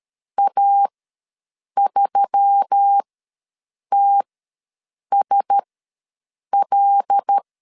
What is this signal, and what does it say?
Morse code, heard as a keyed tone, about 13 wpm, spaced as for 7 wpm, 800 Hz. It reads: A3TSL
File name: LEZIONI / 6 PKX0 / 6 PKX0 ws 12wpm 3448